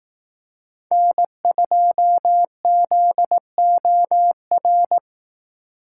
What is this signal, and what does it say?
Morse code, heard as a keyed tone, about 19 wpm, 700 Hz. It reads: N2ZOR
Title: Speed Racing - USA Call Signs 15wpm